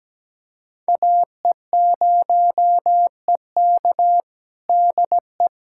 Morse code, heard as a keyed tone, about 17 wpm, 700 Hz. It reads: AE0EK DE
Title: USA Callsign DE Callsign - 5 Characters - 17wpm